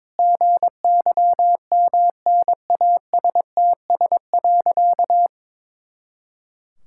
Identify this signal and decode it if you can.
Morse code, heard as a keyed tone, about 22 wpm, 700 Hz. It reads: GYMNASTS.
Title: Pangram Challenge v2 - 22wpm